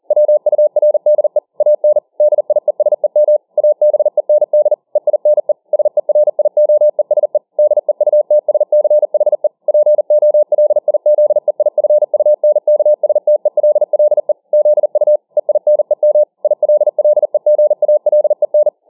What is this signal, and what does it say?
Morse code, heard as a keyed tone, about 40 wpm, 600 Hz. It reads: WURDE AN DIESEM ABEND EINE SERIOESE DEUTSCHE POLIZEIFUNKSTELLE ZU EINEM ILLEGALEN